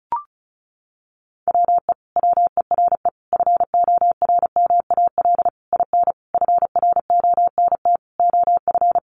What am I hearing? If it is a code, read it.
Morse code, heard as a keyed tone, about 34 wpm, 700 Hz. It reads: WE WERE FORMAL IN FRONT OF
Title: Sentences from Top 1300 Words 35wpm